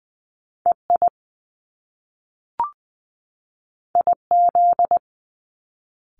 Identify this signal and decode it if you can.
Morse code, heard as a keyed tone, about 19 wpm, 700 Hz. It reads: EI IZ
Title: Top Two Letter Combinations in Words 20wpm